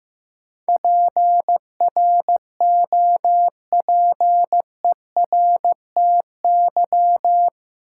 Morse code, heard as a keyed tone, about 15 wpm, 700 Hz. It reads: PROPERTY
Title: Sentences from Top 1000 Words - New Word Drill - 15wpm